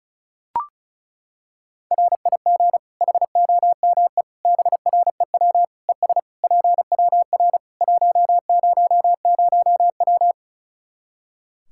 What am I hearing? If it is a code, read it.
Morse code, heard as a keyed tone, about 36 wpm, 700 Hz. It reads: RIG HOME BREW ES PWR 100W
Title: Practice QSO - Exchange 2 - Info - 35wpm